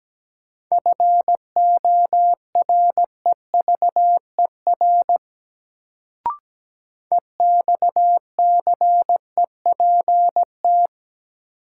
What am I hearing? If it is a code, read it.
Morse code, heard as a keyed tone, about 17 wpm, 700 Hz. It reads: FOREVER EXCEPT
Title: Sentences from Top 1600 Words - New Word Drill Alt - 17wpm